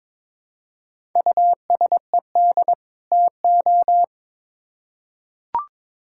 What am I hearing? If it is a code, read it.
Morse code, heard as a keyed tone, about 22 wpm, 700 Hz. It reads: USED TO
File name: Build A Phrase - 2-Word Phrases - 22wpm